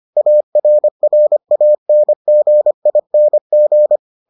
Morse code, heard as a keyed tone, about 25 wpm, 600 Hz. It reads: ARRANGING